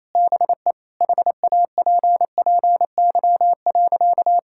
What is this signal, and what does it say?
Morse code, heard as a keyed tone, about 28 wpm, 700 Hz. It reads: BE HAPPY.
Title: Sentences from Top 1000 Words - Review Part 5 - 28wpm